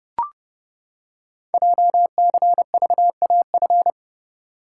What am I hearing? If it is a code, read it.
Morse code, heard as a keyed tone, about 30 wpm, 700 Hz. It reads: JCVAF
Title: Code Groups - 5 Letters - 30wpm